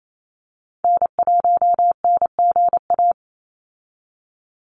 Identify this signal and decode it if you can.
Morse code, heard as a keyed tone, about 28 wpm, 700 Hz. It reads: N1NGA
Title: USA Call Signs 28wpm